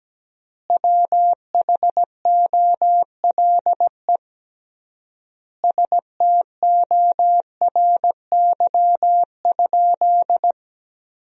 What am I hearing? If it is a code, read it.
Morse code, heard as a keyed tone, about 17 wpm, 700 Hz. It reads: WHOLE STORY?
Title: 3x Word-Spacing - Sentences from Top 400 Words - 17wpm